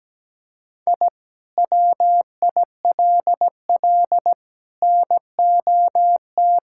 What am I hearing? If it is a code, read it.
Morse code, heard as a keyed tone, about 17 wpm, 700 Hz. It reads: I WILL NOT
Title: Sentences from Top 1000 Words - Review Part 6 - 17wpm